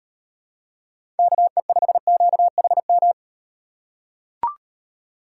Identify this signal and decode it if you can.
Morse code, heard as a keyed tone, about 36 wpm, 700 Hz. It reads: KE5QHM